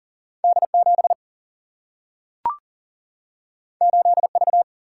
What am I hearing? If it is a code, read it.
Morse code, heard as a keyed tone, about 39 wpm, 700 Hz. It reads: D 7 8 V
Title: Single-Letter Single-Number 40wpm